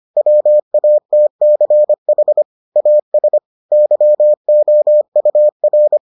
Morse code, heard as a keyed tone, about 25 wpm, 600 Hz. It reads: WATCH AS YOUR